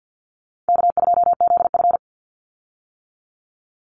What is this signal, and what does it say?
Morse code, heard as a keyed tone, about 50 wpm, 700 Hz. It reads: K2ZF
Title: Easy Call Signs - 50wpm